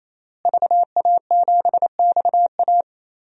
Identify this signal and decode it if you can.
Morse code, heard as a keyed tone, about 28 wpm, 700 Hz. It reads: VA7XA
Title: DXCC-100 Call Signs - 5 Characters - Repeated 3x - 28wpm